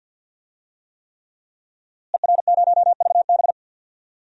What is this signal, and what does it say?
Morse code, heard as a keyed tone, about 52 wpm, 700 Hz. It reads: ER0VB
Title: DXCC-100 Call Signs - 5 Characters - Repeated 3x - 50wpm